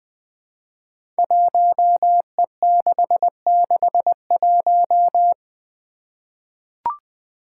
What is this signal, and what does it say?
Morse code, heard as a keyed tone, about 20 wpm, 700 Hz. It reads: 1E661
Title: Five Challenging Letter-Number Combinations 20wpm